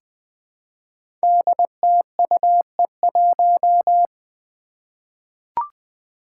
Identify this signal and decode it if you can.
Morse code, heard as a keyed tone, about 20 wpm, 700 Hz. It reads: DTUE1
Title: Five Challenging Letter-Number Combinations 20wpm